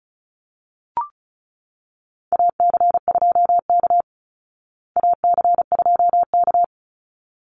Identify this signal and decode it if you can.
Morse code, heard as a keyed tone, about 34 wpm, 700 Hz. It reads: AC2K AC2K
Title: Call Signs Repeated 3x - 35wpm